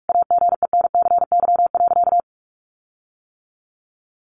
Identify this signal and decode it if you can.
Morse code, heard as a keyed tone, about 44 wpm, 700 Hz. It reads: AGENCY.
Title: Sentences from Top 1300 Words 45wpm